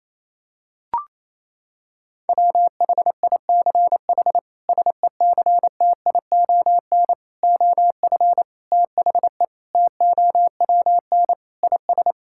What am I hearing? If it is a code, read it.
Morse code, heard as a keyed tone, about 28 wpm, 700 Hz. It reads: WHICH SECTION OF THE TOWN IS
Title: Sentences from Top 1000 Words - Review Part 5 - 28wpm